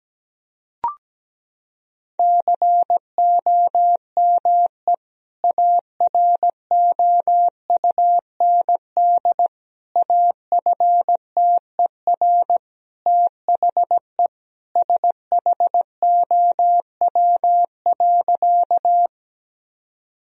Morse code, heard as a keyed tone, about 17 wpm, 700 Hz. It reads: COME AROUND AFTER THE SHOW.
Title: Sentences from Top 1000 Words - Review Part 3 - 17wpm